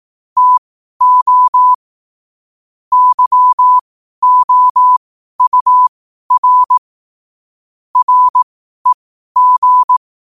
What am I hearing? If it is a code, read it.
Morse code, heard as a keyed tone, about 18 wpm, spaced as for 12 wpm, 1000 Hz. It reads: TO YOUR REG
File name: qotdfiles / 12wpm